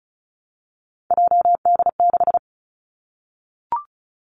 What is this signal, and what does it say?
Morse code, heard as a keyed tone, about 36 wpm, 700 Hz. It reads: JD6
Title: Three Challenging Letter-Number Combinations 35wpm